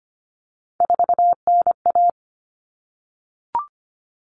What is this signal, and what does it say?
Morse code, heard as a keyed tone, about 25 wpm, 700 Hz. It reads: 4NA
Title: Common Cut Numbers - 3-Digit 25wpm